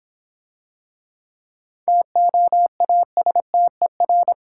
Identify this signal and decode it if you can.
Morse code, heard as a keyed tone, about 26 wpm, 700 Hz. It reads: TOASTER